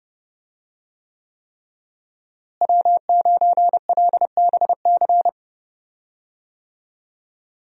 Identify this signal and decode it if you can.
Morse code, heard as a keyed tone, about 30 wpm, 700 Hz. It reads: W9LBC
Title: USA Call Signs - 5 Characters - Repeated 3x - 30wpm